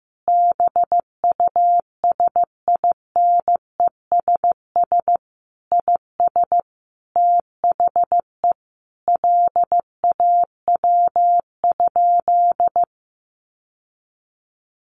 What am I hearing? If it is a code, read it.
Morse code, heard as a keyed tone, about 15 wpm, 700 Hz. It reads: BUSINESS IS THE LAW?
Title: Sentences from Top 600 Words 15wpm